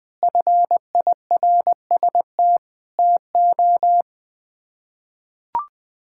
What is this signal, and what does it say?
Morse code, heard as a keyed tone, about 20 wpm, 700 Hz. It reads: FIRST TO
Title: Sets of 6 Words - 20wpm